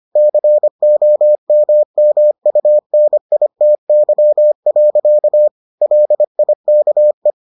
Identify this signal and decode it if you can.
Morse code, heard as a keyed tone, about 25 wpm, 600 Hz. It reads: COMMUNITY. LIKE